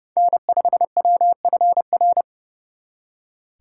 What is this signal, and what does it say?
Morse code, heard as a keyed tone, about 35 wpm, 700 Hz. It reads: N5WFR
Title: Speed Racing - USA Call Signs 30wpm